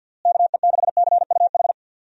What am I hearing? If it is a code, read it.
Morse code, heard as a keyed tone, about 51 wpm, 700 Hz. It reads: KE6CUH
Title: USA Call Signs - 6 Characters - Repeated 3x - 50wpm